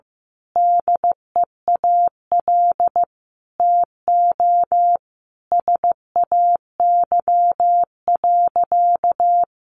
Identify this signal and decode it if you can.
Morse code, heard as a keyed tone, about 14 wpm, 700 Hz. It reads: DEAL TO SAY.
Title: Sentences from Top 1000 Words 15wpm